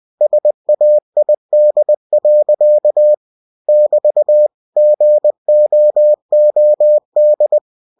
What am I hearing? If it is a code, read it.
Morse code, heard as a keyed tone, about 20 wpm, 600 Hz. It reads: SAID. <BT> GOOD
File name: cw-books / winnie-the-pooh / winnie-20wpm-page-0004